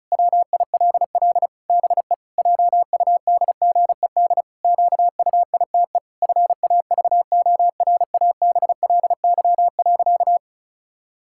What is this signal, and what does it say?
Morse code, heard as a keyed tone, about 36 wpm, 700 Hz. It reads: WILL BE JUDGED QUITE FAVORABLY.
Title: Pangram Challenge v2 - 35wpm